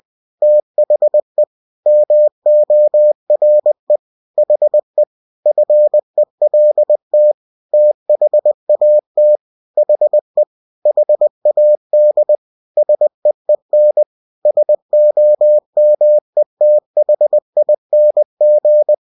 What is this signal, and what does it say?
Morse code, heard as a keyed tone, about 20 wpm, 600 Hz. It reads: THE MORE HE FELT THAT HE HAD SEEN SOMETHING
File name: cw-books / winnie-the-pooh / winnie-20wpm-page-0031